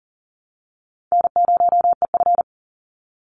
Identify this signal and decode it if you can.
Morse code, heard as a keyed tone, about 40 wpm, 700 Hz. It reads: N0EF
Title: Easy Call Signs Repeated 2x - 40wpm